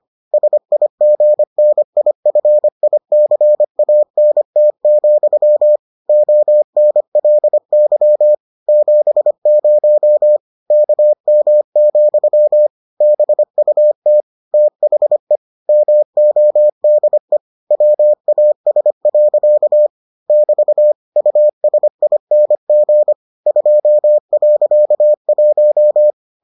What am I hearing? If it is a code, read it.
Morse code, heard as a keyed tone, about 25 wpm, 600 Hz. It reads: SIGNIFICANT, ONLY 70 KM, BUT THE MODE WAS. <BT> USING 2.1